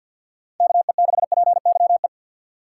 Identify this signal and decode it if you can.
Morse code, heard as a keyed tone, about 53 wpm, 690 Hz. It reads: KE6PYE
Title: USA Call Signs - 2x Character Spacing - 50wpm